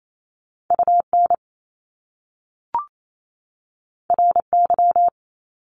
Morse code, heard as a keyed tone, about 28 wpm, 700 Hz. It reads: UN RY
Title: Top Two Letter Combinations in Words 28wpm